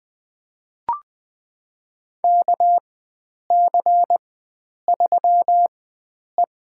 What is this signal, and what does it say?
Morse code, heard as a keyed tone, about 20 wpm, spaced as for 9 wpm, 700 Hz. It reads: KC3E
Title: USA Call Signs - 4x Character Spacing - 20wpm